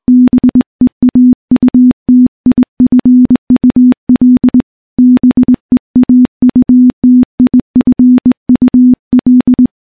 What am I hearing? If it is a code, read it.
Morse code, heard as a keyed tone, about 20 wpm, 250 Hz. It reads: BEAUTIFUL BEAUTIFUL
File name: morse / morse-training-20WPM